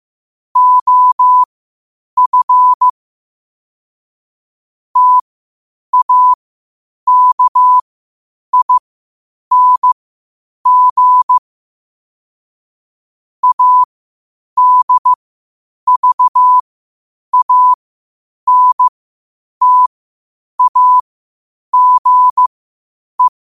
Morse code, heard as a keyed tone, about 15 wpm, spaced as for 8 wpm, 1000 Hz. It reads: OF TAKING ADVANTAGE
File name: qotdfiles / 8wpm